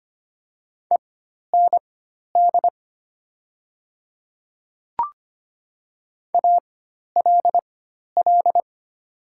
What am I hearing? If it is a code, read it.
Morse code, heard as a keyed tone, about 25 wpm, spaced as for 12 wpm, 700 Hz. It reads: END ALL
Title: 4x Character Spacing - Top 300 Words - 25wpm